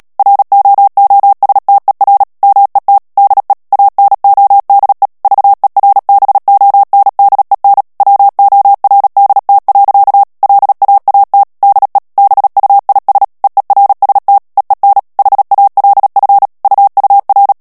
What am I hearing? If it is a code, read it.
Morse code, heard as a keyed tone, about 37 wpm, 800 Hz. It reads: ROOSTER MET DE ANODE VERBONDEN WORDT. LAAT DE BUIS EERST EEN HALF UUR
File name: morse mp3 / HOLLANDAIS texte clair 2 vitesse 35 mots minute